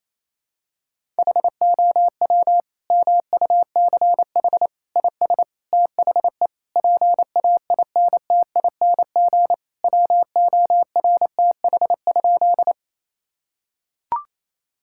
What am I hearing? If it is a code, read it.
Morse code, heard as a keyed tone, about 27 wpm, 700 Hz. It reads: HOW MUCH IS THE PAINTING WORTH?
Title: Sentences from Top 1000 Words - Review Part 6 - 28wpm